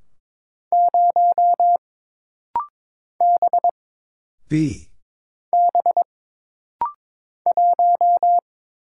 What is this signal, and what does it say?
Morse code, heard as a keyed tone, about 22 wpm, 700 Hz. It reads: 0 B B 1
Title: Single Letter-Number - Rapid-Fire 22wpm